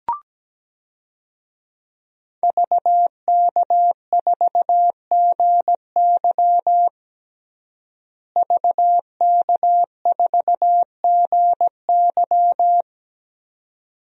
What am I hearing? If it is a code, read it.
Morse code, heard as a keyed tone, about 17 wpm, 700 Hz. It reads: VK4GY VK4GY
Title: DXCC-100 Call Signs - 5 Characters - Repeated 2x - 17wpm